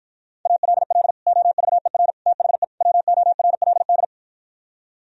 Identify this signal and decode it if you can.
Morse code, heard as a keyed tone, about 53 wpm, 700 Hz. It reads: ALL OVER THE WORLD